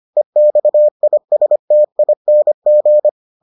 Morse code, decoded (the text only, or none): EXISTING